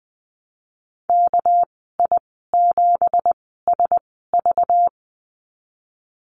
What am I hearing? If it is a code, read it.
Morse code, heard as a keyed tone, about 19 wpm, spaced as for 14 wpm, 700 Hz. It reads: KI7SV
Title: 200 Lesson 32g - Letter k - Call Signs 20wpm